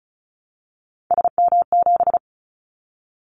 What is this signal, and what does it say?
Morse code, heard as a keyed tone, about 36 wpm, 700 Hz. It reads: SM7
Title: Callsign Prefixes for Top 25 DXCC - 35wpm